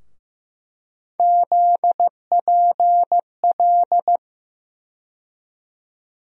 Morse code, heard as a keyed tone, about 15 wpm, 700 Hz. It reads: ZPL
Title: USA Call Sign Postfixes - 3 Letters 15wpm